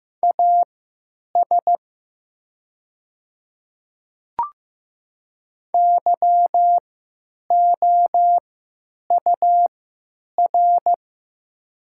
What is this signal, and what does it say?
Morse code, heard as a keyed tone, about 15 wpm, spaced as for 9 wpm, 700 Hz. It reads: AS YOUR
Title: 3x Character Spacing - Top 100 Words - 15wpm